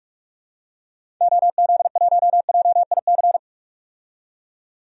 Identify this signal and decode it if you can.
Morse code, heard as a keyed tone, about 46 wpm, 700 Hz. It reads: OZ1JIC
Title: DXCC-100 Call Signs - 6 Characters - 45wpm